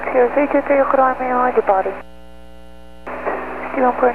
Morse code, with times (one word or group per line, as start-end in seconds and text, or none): none